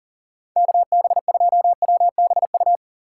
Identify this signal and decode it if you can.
Morse code, heard as a keyed tone, about 52 wpm, 700 Hz. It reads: KB2WBU